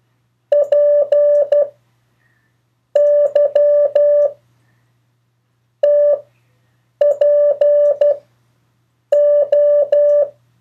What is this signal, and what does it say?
Morse code, heard as a keyed tone, about 12 wpm, 580 Hz. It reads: P Y T P O
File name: 3of4-words-2